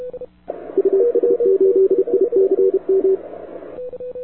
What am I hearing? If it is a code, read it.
Morse code, heard as a keyed tone, about 31 wpm, 380 Hz, with radio noise. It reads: US8ICM
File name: wwcw08 / k5zd wwcw08 2000z